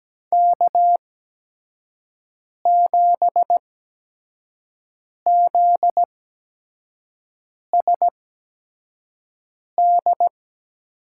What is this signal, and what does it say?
Morse code, heard as a keyed tone, about 17 wpm, spaced as for 5 wpm, 700 Hz. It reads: K7ZSD
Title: USA Call Signs - 5 Characters - 8x Character Spacing - 17wpm